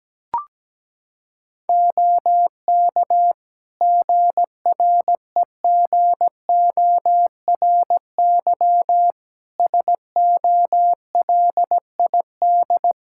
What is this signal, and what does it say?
Morse code, heard as a keyed tone, about 17 wpm, 700 Hz. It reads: OK GREGORY SOLID
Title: Practice QSO - Exchange 3 - More Info - 17wpm